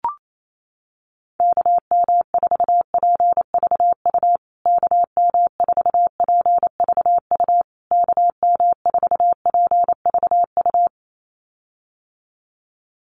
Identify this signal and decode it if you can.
Morse code, heard as a keyed tone, about 28 wpm, 700 Hz. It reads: KM4PVU KM4PVU KM4PVU
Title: USA Call Signs Repeated 3x 28wpm